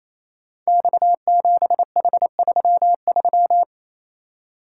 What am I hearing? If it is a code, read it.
Morse code, heard as a keyed tone, about 28 wpm, 700 Hz. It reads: X7H33
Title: Code Groups - 5 Letter:Numbers - 28wpm